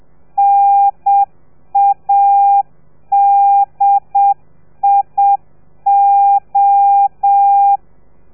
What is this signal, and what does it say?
Morse code, heard as a keyed tone, about 7 wpm, 800 Hz. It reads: NADIO